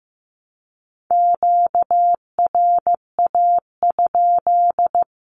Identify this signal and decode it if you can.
Morse code, heard as a keyed tone, about 15 wpm, 700 Hz. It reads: QRA?
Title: Q-Codes 15wpm